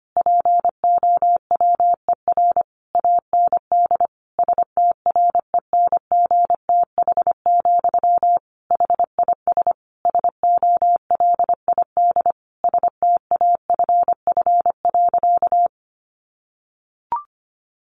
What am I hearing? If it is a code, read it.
Morse code, heard as a keyed tone, about 25 wpm, 700 Hz. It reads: POWER AND STRENGTH, HIS SOLID STAFF.
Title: Sentences from Top 1500 Words 25wpm